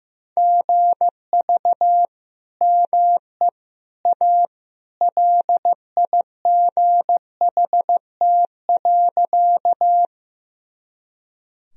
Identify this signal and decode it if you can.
Morse code, heard as a keyed tone, about 16 wpm, 700 Hz. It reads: GV ME A LIGHT.
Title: Sentences from Top 200 Words and Abbreviations 15wpm